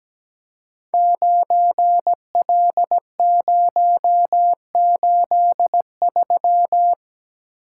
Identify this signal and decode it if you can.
Morse code, heard as a keyed tone, about 17 wpm, 700 Hz. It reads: 9L083
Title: Code Groups - 5 Letter:Numbers - 17wpm